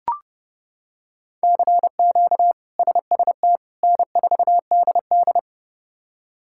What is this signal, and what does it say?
Morse code, heard as a keyed tone, about 30 wpm, 700 Hz. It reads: CQ SST N4DD
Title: K1USN SST QSO Practice - 30wpm